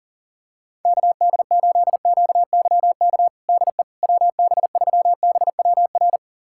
Morse code, heard as a keyed tone, about 37 wpm, 700 Hz. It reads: KD8QYK DE WB3BWR
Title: USA Callsign DE Callsign 40wpm